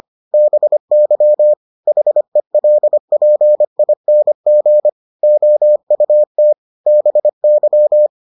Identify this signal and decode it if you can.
Morse code, heard as a keyed tone, about 25 wpm, 600 Hz. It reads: BY HELPING OUT BY